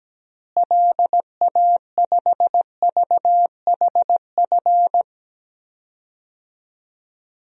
Morse code, heard as a keyed tone, about 17 wpm, 700 Hz. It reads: LA5VHF
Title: DXCC-100 Call Signs - 6 Characters - Repeated 3x - 17wpm